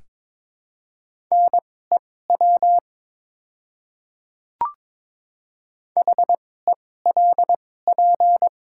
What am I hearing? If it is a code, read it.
Morse code, heard as a keyed tone, about 22 wpm, spaced as for 16 wpm, 700 Hz. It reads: NEW HELP
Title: 2x Character Spacing - Top 300 Words - 22wpm